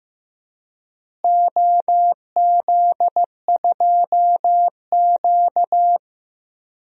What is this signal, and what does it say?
Morse code, heard as a keyed tone, about 16 wpm, 700 Hz. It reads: OZ2Q
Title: DXCC-100 Call Signs - 4 Characters - Repeated 2x - 15wpm